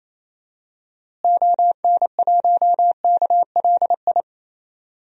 Morse code, heard as a keyed tone, about 28 wpm, 700 Hz. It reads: ON1KLI
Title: DXCC-100 Call Signs - 6 Characters - Repeated 3x - 28wpm